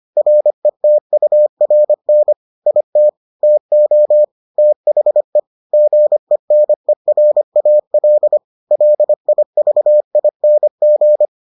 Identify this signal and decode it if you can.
Morse code, heard as a keyed tone, about 25 wpm, 600 Hz. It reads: RETURN IT TO THE GENERAL LIVING